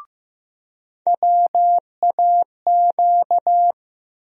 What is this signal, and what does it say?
Morse code, heard as a keyed tone, about 15 wpm, 700 Hz. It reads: WAQ
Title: USA Call Sign Postfixes - 3 Letters 15wpm